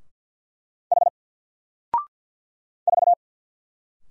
Morse code, heard as a keyed tone, about 50 wpm, 700 Hz. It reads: H 4
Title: Single Letter-Number 50wpm